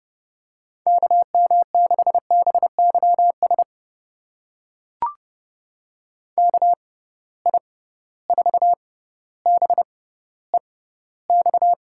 Morse code, heard as a keyed tone, about 30 wpm, spaced as for 15 wpm, 700 Hz. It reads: KM6BYS KI4BEX
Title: USA Call Signs - 6 Characters - 6x Character Spacing - 30wpm